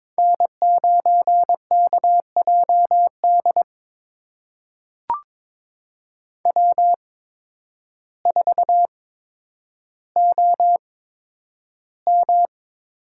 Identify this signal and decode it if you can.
Morse code, heard as a keyed tone, about 22 wpm, spaced as for 10 wpm, 700 Hz. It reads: N9KJD W4OM
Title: USA Call Signs - 5 Characters - 8x Character Spacing - 22wpm